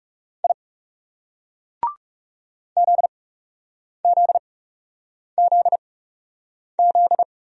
Morse code, heard as a keyed tone, about 35 wpm, spaced as for 8 wpm, 700 Hz. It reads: I ZZZZ
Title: Speed-Racing Single Letters 30wpm